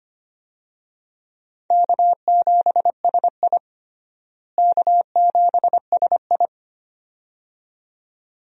Morse code, heard as a keyed tone, about 25 wpm, 700 Hz. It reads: K7SI K7SI
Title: USA Call Signs - 4 Characters - Repeated 2x - 25wpm